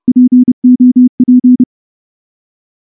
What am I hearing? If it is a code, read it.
Morse code, heard as a keyed tone, about 28 wpm, 250 Hz. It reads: POP
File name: morse / morse-training-30WPM